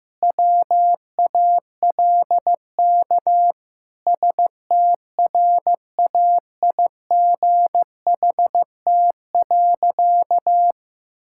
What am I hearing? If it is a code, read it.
Morse code, heard as a keyed tone, about 16 wpm, 700 Hz. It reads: WALK STRAIGHT.
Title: Sentences from Top 1000 Words - Review Part 6 - 15wpm